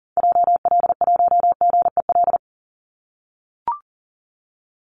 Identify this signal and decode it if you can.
Morse code, heard as a keyed tone, about 40 wpm, 700 Hz. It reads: JL1GEL
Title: Call Signs - 40wpm